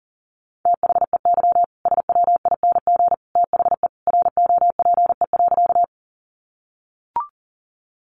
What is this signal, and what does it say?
Morse code, heard as a keyed tone, about 40 wpm, 700 Hz. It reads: THEY SWING THE ROPE.
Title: Easy Phrases 40wpm